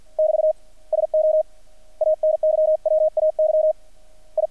Morse code, heard as a keyed tone, about 50 wpm, 620 Hz. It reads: <BT> SO ANYWAY I
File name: archived / CRSnet